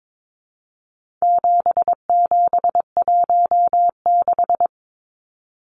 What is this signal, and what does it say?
Morse code, heard as a keyed tone, about 22 wpm, 700 Hz. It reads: 7716
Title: Numbers - 4-Digit 22wpm